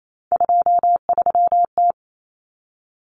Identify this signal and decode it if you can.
Morse code, heard as a keyed tone, about 28 wpm, 700 Hz. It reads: 23T
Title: Common Cut Numbers - 3-Digit 28wpm